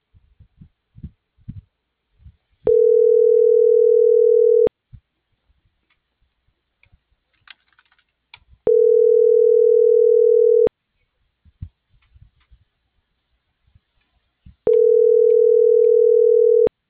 Ringback tone (North America), 440 + 480 Hz.
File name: CENTRAL MAERSK / out adolfo.msk - .28340